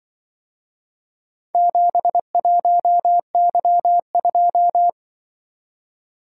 Morse code, heard as a keyed tone, about 25 wpm, 700 Hz. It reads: 71Y2
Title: Speed-Racing - Four Letter-Number Combinations - 20wpm